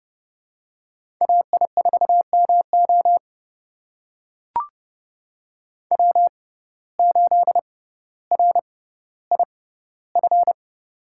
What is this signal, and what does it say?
Morse code, heard as a keyed tone, about 30 wpm, spaced as for 15 wpm, 700 Hz. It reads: AI4MO W8RIF
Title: USA Call Signs - 5 Characters - 6x Character Spacing - 30wpm